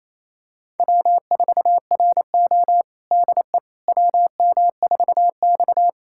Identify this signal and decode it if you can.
Morse code, heard as a keyed tone, about 28 wpm, 690 Hz. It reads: W4RO DE WM4X
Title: USA Callsign DE Callsign - 4 Characters - 28wpm